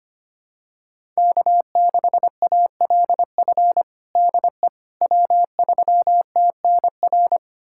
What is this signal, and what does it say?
Morse code, heard as a keyed tone, about 25 wpm, 700 Hz. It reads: K6ALF DE W3TNR
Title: USA Callsign DE Callsign - 5 Characters - 25wpm